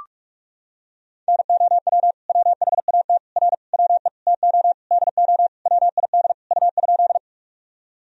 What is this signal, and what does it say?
Morse code, heard as a keyed tone, about 44 wpm, 700 Hz. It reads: NOW WHAT R WE TO DO WID U?
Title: Sentences from Top 100 Words and Abbreviations 45wpm